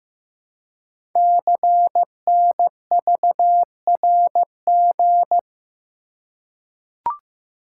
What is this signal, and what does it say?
Morse code, heard as a keyed tone, about 15 wpm, 700 Hz. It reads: CNVRG
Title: Code Groups - 5 Letters - 15wpm